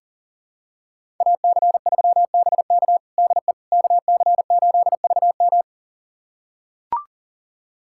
Morse code, heard as a keyed tone, about 40 wpm, 700 Hz. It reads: AC3BK DE KC8VM
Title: USA Callsign DE Callsign - 5 Characters - 40wpm